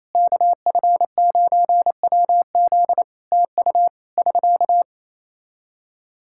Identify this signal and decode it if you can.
Morse code, heard as a keyed tone, about 28 wpm, 700 Hz. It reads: KF9WZ TU <SK>
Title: Practice QSO - Exchange 4 - Ending - 28wpm